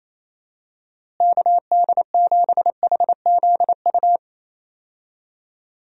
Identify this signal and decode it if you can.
Morse code, heard as a keyed tone, about 28 wpm, 700 Hz. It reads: KD7HZU
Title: USA Call Signs - 6 Characters - Repeated 3x - 28wpm